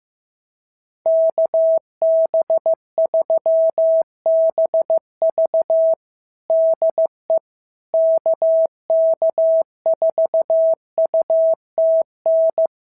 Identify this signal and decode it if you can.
Morse code, heard as a keyed tone, about 14 wpm, 650 Hz. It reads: KB3BV DE KK4UTN
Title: SOTA QSO Practice 15wpm